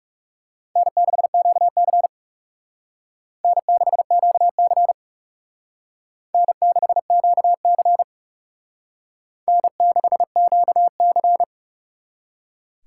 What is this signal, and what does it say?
Morse code, heard as a keyed tone, about 36 wpm, 700 Hz. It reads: N6QC N6QC N6QC N6QC
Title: Speed Racing - USA 1x2 Call Signs 30wpm